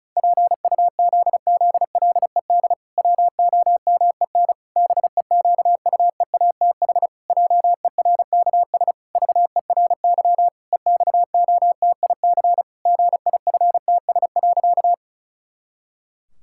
Morse code, heard as a keyed tone, about 36 wpm, 700 Hz. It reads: PUZZLED WOMEN BEQUEATH JERKS VERY EXOTIC GIFTS.
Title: Pangram Challenge v2 - 35wpm